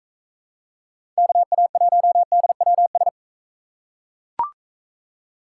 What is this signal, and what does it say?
Morse code, heard as a keyed tone, about 41 wpm, 700 Hz. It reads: KA1DWS